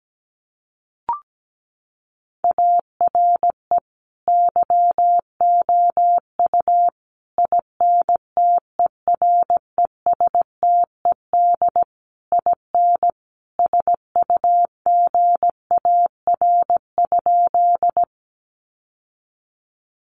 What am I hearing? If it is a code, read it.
Morse code, heard as a keyed tone, about 17 wpm, 700 Hz. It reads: ARE YOU INTERESTED IN SUGAR?
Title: Sentences from Top 1200 Words 17wpm